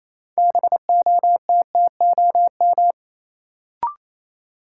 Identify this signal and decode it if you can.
Morse code, heard as a keyed tone, about 28 wpm, 700 Hz. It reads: BOTTOM
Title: Sets of 3 Words - New Word Drill - 28wpm